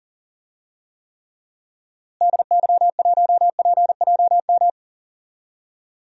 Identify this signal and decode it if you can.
Morse code, heard as a keyed tone, about 38 wpm, 700 Hz. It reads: DY1PJM
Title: DXCC-100 Call Signs - 6 Characters - Repeated 3x - 40wpm